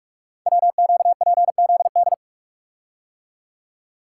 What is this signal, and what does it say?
Morse code, heard as a keyed tone, about 44 wpm, 700 Hz. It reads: WQPZD
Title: Code Groups - 5 Letter:Numbers - 45wpm